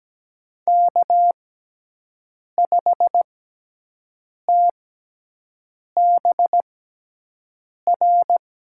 Morse code, heard as a keyed tone, about 17 wpm, spaced as for 6 wpm, 700 Hz. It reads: K5TBR
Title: USA Call Signs - 6x Character Spacing - 17wpm